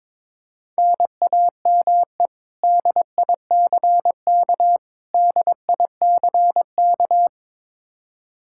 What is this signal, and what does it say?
Morse code, heard as a keyed tone, about 22 wpm, 700 Hz. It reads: NAME DICK DICK
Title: Practice QSO - Exchange 1 - Beginning - 22wpm